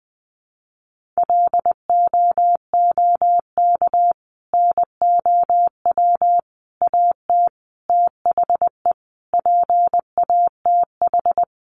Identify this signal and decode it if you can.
Morse code, heard as a keyed tone, about 20 wpm, 700 Hz. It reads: LOOK NOW AT THE PATH
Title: Sentences from Top 1400 Words 20wpm